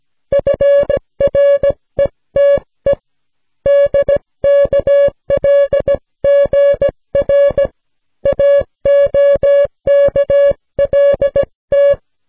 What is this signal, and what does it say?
Morse code, heard as a keyed tone, about 17 wpm, 560 Hz. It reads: URETE DKLGR AOKLT